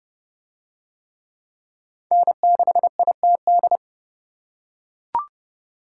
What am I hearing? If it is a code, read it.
Morse code, heard as a keyed tone, about 30 wpm, 700 Hz. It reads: N6ITD